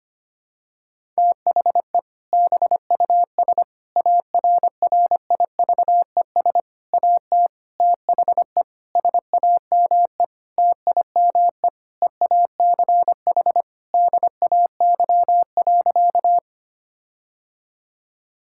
Morse code, heard as a keyed tone, about 25 wpm, 700 Hz. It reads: THE BUS ARRIVES AT THE SAME TIME EACH DAY.
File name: practice / 25wpm / Difficult Sentences with Hints - 25wpm